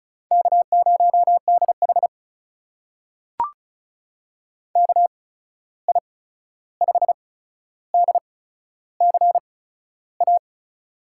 Morse code, heard as a keyed tone, about 36 wpm, spaced as for 13 wpm, 700 Hz. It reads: K0DH KI5DCA
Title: USA Call Signs - 8x Character Spacing - 35wpm